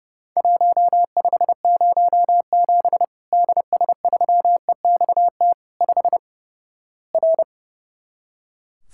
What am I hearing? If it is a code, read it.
Morse code, heard as a keyed tone, about 30 wpm, 700 Hz. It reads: 1507 DS3EXT 5 R
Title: Worked All Europe DX Contest - QTC Exchange - 30wpm